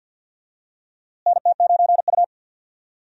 Morse code, heard as a keyed tone, about 52 wpm, 700 Hz. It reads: NT9U
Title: USA Call Signs - 4 Characters - Repeated 3x - 50wpm